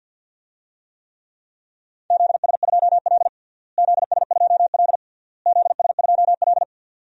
Morse code, heard as a keyed tone, about 52 wpm, 700 Hz. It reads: ZS2L ZS2L ZS2L
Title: DXCC-100 Call Signs - 4 Characters - Repeated 3x - 50wpm